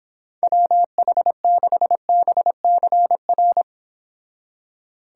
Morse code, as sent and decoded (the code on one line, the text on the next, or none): .-- .... -.... -... -.-. .-.
WH6BCR